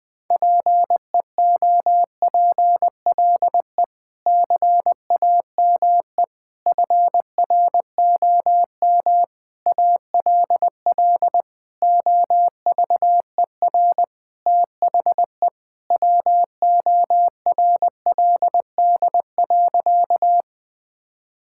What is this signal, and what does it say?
Morse code, heard as a keyed tone, about 20 wpm, 700 Hz. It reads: PEOPLE CAME FROM ALL OVER THE WORLD.
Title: Sentences with 5-Word Phrases - Spoken Beginning - 20wpm